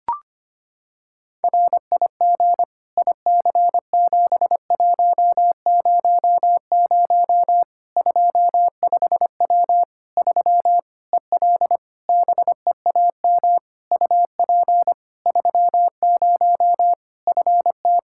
Morse code, heard as a keyed tone, about 25 wpm, 700 Hz. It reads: RIG IC7100 25W 3 EL BEAM UP 30 FT
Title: Practice QSO - Exchange 2 - Info - 25wpm